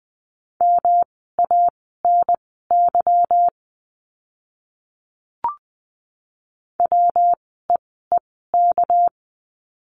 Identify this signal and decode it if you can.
Morse code, heard as a keyed tone, about 20 wpm, spaced as for 15 wpm, 700 Hz. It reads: MANY WEEK
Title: 230 Lesson 37f - Letter x - Words 20wpm